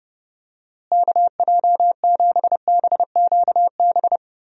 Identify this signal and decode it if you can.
Morse code, heard as a keyed tone, about 30 wpm, 700 Hz. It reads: KJ7BQB
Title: USA Call Signs - 6 Characters - Repeated 3x - 30wpm